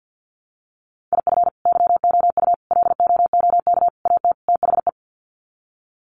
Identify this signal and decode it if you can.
Morse code, heard as a keyed tone, about 50 wpm, 700 Hz. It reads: IF YOU LOOK AT THE
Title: Sets of 5 Words 50wpm